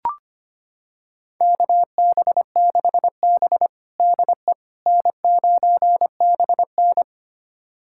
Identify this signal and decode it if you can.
Morse code, heard as a keyed tone, about 25 wpm, 700 Hz. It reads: KB6B DE N9BN
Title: USA Callsign DE Callsign - 4 Characters - 25wpm